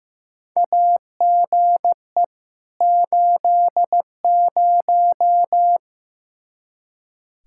Morse code, heard as a keyed tone, about 15 wpm, 700 Hz. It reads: AGE 80
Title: Practice QSO - Exchange 3 - More Info - 15wpm